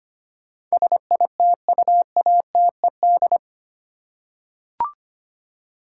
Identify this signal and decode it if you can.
Morse code, heard as a keyed tone, about 25 wpm, 700 Hz. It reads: SITUATED
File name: Eight Letter Words - 25wpm